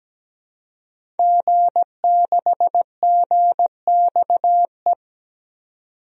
Morse code, heard as a keyed tone, about 17 wpm, 700 Hz. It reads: G6GXE
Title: DXCC-100 Call Signs - 5 Characters - 17wpm